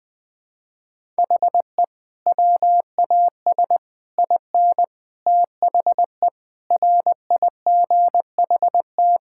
Morse code, heard as a keyed tone, about 20 wpm, 700 Hz. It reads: HE WAS IN THE RIGHT
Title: Sentences with 9-Word Phrases - 20wpm